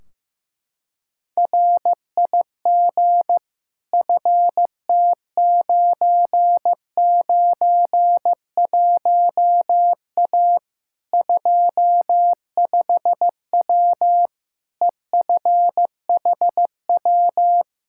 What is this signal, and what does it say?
Morse code, heard as a keyed tone, about 15 wpm, 700 Hz. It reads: RIG FT991A 25W EFHW
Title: Practice QSO - Exchange 2 - Info - 15wpm